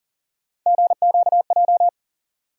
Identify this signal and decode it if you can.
Morse code, heard as a keyed tone, about 40 wpm, 700 Hz. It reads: GQJ
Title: USA Call Sign Postfixes - 3 Letters 40wpm